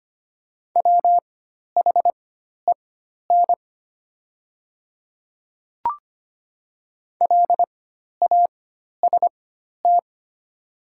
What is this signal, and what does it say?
Morse code, heard as a keyed tone, about 25 wpm, spaced as for 12 wpm, 700 Hz. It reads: WHEN LAST
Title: 4x Character Spacing - Top 500 Words - 25wpm